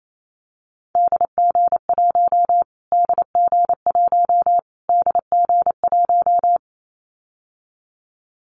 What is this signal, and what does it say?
Morse code, heard as a keyed tone, about 28 wpm, 700 Hz. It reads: DG1 DG1 DG1
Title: Callsign Prefixes for Top 25 DXCC - Repeated 3x - 28wpm